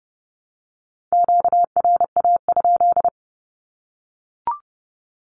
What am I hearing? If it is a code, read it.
Morse code, heard as a keyed tone, about 30 wpm, 700 Hz. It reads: QRA?
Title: Q-Codes Repeated 3x 30wpm